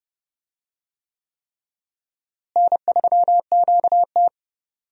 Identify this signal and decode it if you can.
Morse code, heard as a keyed tone, about 29 wpm, 700 Hz. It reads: N3QT
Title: USA Call Signs - 4 Characters - Repeated 3x - 30wpm